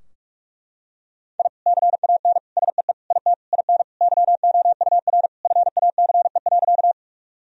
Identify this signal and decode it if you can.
Morse code, heard as a keyed tone, about 41 wpm, 700 Hz. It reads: I CAN SEE IT IN YOUR FACE.
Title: Sentences from Top 1000 Words - Review Part 2 - 45wpm